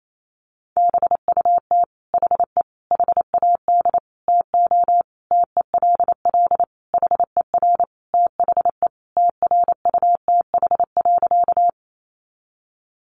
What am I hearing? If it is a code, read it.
Morse code, heard as a keyed tone, about 28 wpm, 700 Hz. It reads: BUT HE HAD TO TELL HER THE TRUTH.
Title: Sentences from Top 800 Words 28wpm